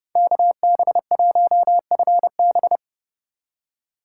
Morse code, heard as a keyed tone, about 30 wpm, 700 Hz. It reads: KB1FB
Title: USA Call Signs - 5 Characters - 2x Character Spacing - 30wpm